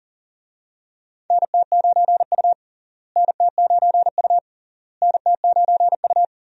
Morse code, heard as a keyed tone, about 40 wpm, 700 Hz. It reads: NT9U NT9U NT9U
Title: USA Call Signs - 4 Characters - Repeated 3x - 40wpm